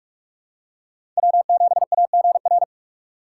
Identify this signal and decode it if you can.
Morse code, heard as a keyed tone, about 43 wpm, 700 Hz. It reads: W7AGR
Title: USA Call Signs - 5 Characters - 3x Character Spacing - 45wpm